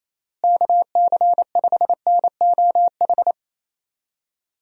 Morse code, heard as a keyed tone, about 28 wpm, 700 Hz. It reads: KC5NOH
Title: USA Call Signs - 6 Characters - Repeated 3x - 28wpm